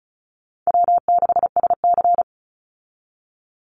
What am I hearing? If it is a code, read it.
Morse code, heard as a keyed tone, about 34 wpm, 700 Hz. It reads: W6SC
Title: Call Signs Repeated 3x - 35wpm